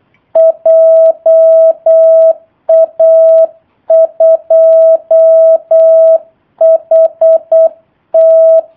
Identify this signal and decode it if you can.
Morse code, heard as a keyed tone, about 8 wpm, 650 Hz. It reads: JA2HT
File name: jh8cbh / 2019cw / sp03- w